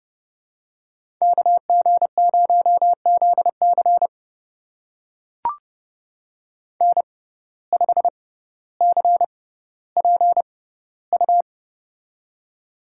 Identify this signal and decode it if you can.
Morse code, heard as a keyed tone, about 30 wpm, spaced as for 15 wpm, 700 Hz. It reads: KG0ZC N5CPU
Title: USA Call Signs - 5 Characters - 6x Character Spacing - 30wpm